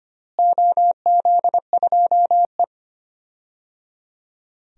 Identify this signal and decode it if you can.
Morse code, heard as a keyed tone, about 25 wpm, 700 Hz. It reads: OZ2E
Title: DXCC-100 Call Signs - 4 Characters - Repeated 3x - 25wpm